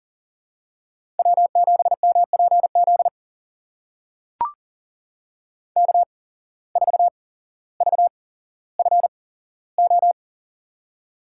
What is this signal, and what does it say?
Morse code, heard as a keyed tone, about 40 wpm, spaced as for 17 wpm, 700 Hz. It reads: W7MPZ K4VFO
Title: USA Call Signs - 5 Characters - 8x Character Spacing - 40wpm